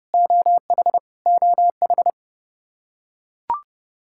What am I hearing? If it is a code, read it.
Morse code, heard as a keyed tone, about 30 wpm, 700 Hz. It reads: OH OH
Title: K1USN SST QSO Practice 30wpm